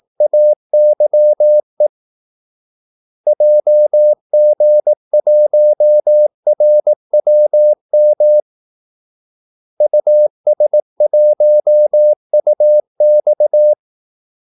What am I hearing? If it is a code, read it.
Morse code, heard as a keyed tone, about 18 wpm, 600 Hz. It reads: AYE JG1RWM US1UX
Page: Kaldesignaler 18 wpm